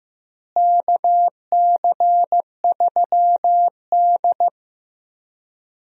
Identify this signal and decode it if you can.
Morse code, heard as a keyed tone, about 15 wpm, 700 Hz. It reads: KC3D
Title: USA Call Signs - 4 Characters - Repeated 3x - 15wpm